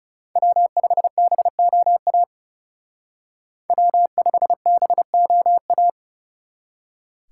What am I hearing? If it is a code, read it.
Morse code, heard as a keyed tone, about 35 wpm, 700 Hz. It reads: W5BOA W5BOA
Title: Speed Racing - USA 1x3 Call Signs 30wpm